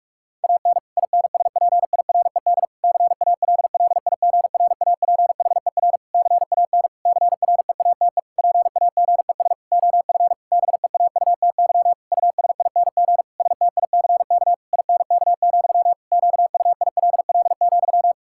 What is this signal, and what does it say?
Morse code, heard as a keyed tone, about 46 wpm, 700 Hz. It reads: AN INSPIRED CALLIGRAPHER CAN CREATE PAGES OF BEAUTY USING STICK INK, QUILL,
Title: Pangram Challenge v2 - 45wpm